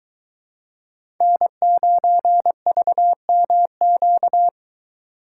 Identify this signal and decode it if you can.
Morse code, heard as a keyed tone, about 22 wpm, 700 Hz. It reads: N9VMQ